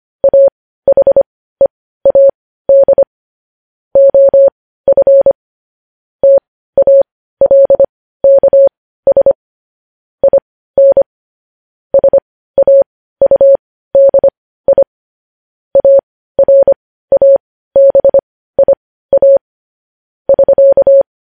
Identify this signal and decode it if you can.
Morse code, heard as a keyed tone, about 25 wpm, spaced as for 15 wpm, 550 Hz. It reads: AHEAD OF TALKS IN SAUDI ARABIA <SK>
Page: News headlines in Morse code at 15 wpm